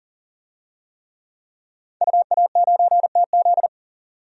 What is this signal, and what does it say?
Morse code, heard as a keyed tone, about 40 wpm, 700 Hz. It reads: UA9TZ
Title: DXCC-100 Call Signs - 5 Characters - 40wpm